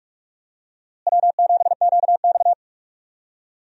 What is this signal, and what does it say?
Morse code, heard as a keyed tone, about 46 wpm, 700 Hz. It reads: W7QX
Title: USA Call Signs - 4 Characters - Repeated 2x - 45wpm